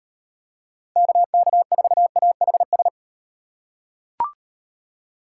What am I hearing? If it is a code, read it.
Morse code, heard as a keyed tone, about 36 wpm, 700 Hz. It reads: KK4AHS